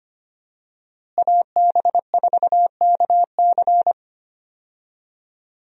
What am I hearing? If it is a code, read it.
Morse code, heard as a keyed tone, about 25 wpm, 700 Hz. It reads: AB4KC
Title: USA Call Signs - 5 Characters - Repeated 2x - 25wpm